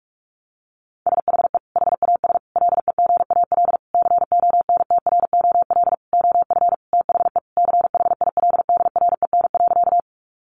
Morse code, heard as a keyed tone, about 44 wpm, 700 Hz. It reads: SHE HAS LEGAL CONTROL OF THE CHILDREN.
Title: Sentences from Top 1200 Words 45wpm